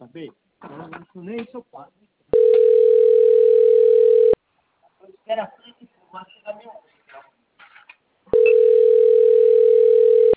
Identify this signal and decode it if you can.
Ringback tone (North America), 440 + 480 Hz, with speech.
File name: rg-1600-unknown- - - .53684